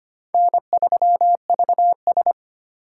Morse code, heard as a keyed tone, about 25 wpm, 700 Hz. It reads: N3VS
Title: USA Call Signs - 4 Characters - 4x Character Spacing - 25wpm